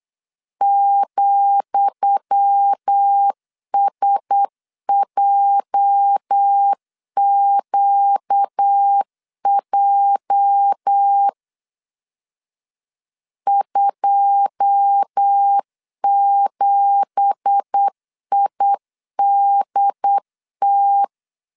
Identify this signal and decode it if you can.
Morse code, heard as a keyed tone, about 8 wpm, 800 Hz. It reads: ,SJQJ 27IDT
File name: LEZIONI / 10 SIGNS / 10D COMA 8wpm 3749